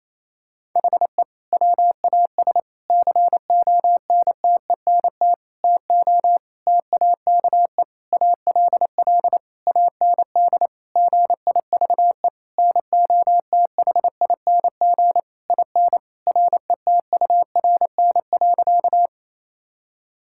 Morse code, heard as a keyed tone, about 28 wpm, 700 Hz. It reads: HE WAS CONTENT TO TAKE ALL AND GIVE NOTHING IN RETURN.
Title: Sentences from Top 1000 Words - Review Part 6 - 28wpm